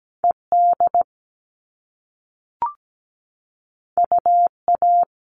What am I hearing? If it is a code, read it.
Morse code, heard as a keyed tone, about 17 wpm, 700 Hz. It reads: ED UA
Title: All Cut Numbers - 2-Digit 17wpm